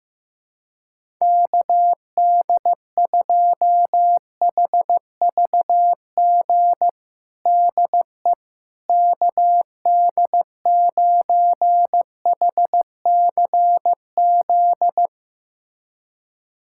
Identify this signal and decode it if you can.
Morse code, heard as a keyed tone, about 16 wpm, 700 Hz. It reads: KD2HVG DE KD9HCZ
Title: USA Callsign DE Callsign - 6 Characters - 15wpm